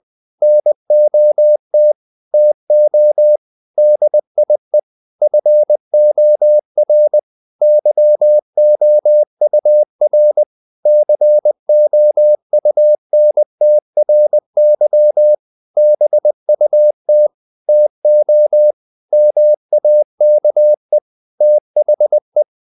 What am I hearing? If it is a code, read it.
Morse code, heard as a keyed tone, about 20 wpm, 600 Hz. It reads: NOT TO DIE FOR YOUR COUNTRY BUT TO MAKE THE
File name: CW / Practice / EnglishQuotations0320